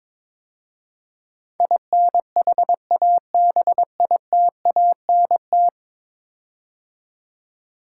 Morse code, heard as a keyed tone, about 22 wpm, 700 Hz. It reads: INHABITANT
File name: practice / 22wpm / 10 Letter Words - First 3 Letters Given - 22wpm